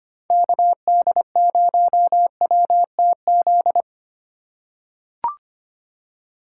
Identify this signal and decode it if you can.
Morse code, heard as a keyed tone, about 25 wpm, 700 Hz. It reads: KD0WTZ
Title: USA Call Signs - 6 Characters - 3x Character Spacing - 25wpm